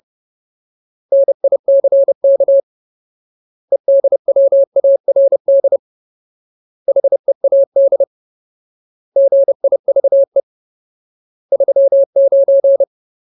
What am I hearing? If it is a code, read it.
Morse code, heard as a keyed tone, about 30 wpm, 550 Hz. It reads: NICK EDWARD HEAD GIVE 39
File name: Ultimate Word List / 30WPM Ultimate